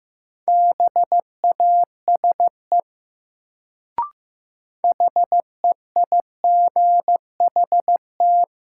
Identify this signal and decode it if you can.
Morse code, heard as a keyed tone, about 15 wpm, 700 Hz. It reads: BASE HEIGHT
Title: Sentences from Top 1600 Words - New Word Drill Alt - 15wpm